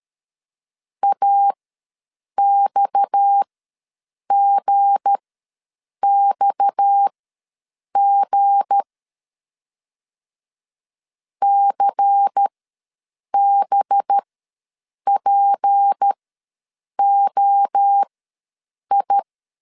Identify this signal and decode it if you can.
Morse code, heard as a keyed tone, about 13 wpm, spaced as for 7 wpm, 800 Hz. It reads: AXGXG CBPOI
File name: LEZIONI / 7 GF9Z / 7 G ws 12wpm 3140